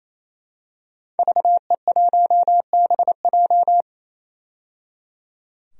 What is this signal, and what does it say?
Morse code, heard as a keyed tone, about 28 wpm, 700 Hz. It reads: VE1BJ
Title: DXCC-100 Call Signs - 5 Characters - 28wpm